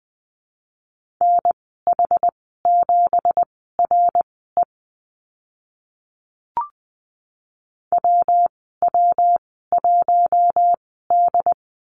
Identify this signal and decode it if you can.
Morse code, heard as a keyed tone, about 20 wpm, spaced as for 14 wpm, 700 Hz. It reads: NH7RE WW1D
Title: 167 Lesson 27g - Number 7 - Call Signs 20wpm